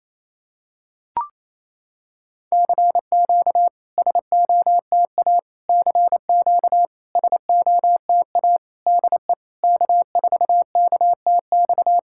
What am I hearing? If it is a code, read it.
Morse code, heard as a keyed tone, about 28 wpm, 700 Hz. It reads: CQ SOTA CQ SOTA DE K4KTX
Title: SOTA QSO Practice 28wpm